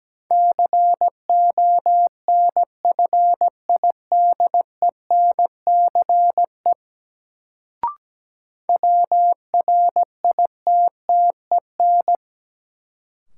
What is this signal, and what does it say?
Morse code, heard as a keyed tone, about 17 wpm, 700 Hz. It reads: CONFIDENCE WRITTEN
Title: Sentences from Top 1500 Words - New Word Drill - 17wpm